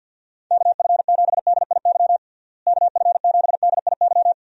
Morse code, heard as a keyed tone, about 46 wpm, 700 Hz. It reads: KF7DIY KF7DIY
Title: USA Call Signs - 6 Characters - Repeated 2x - 50wpm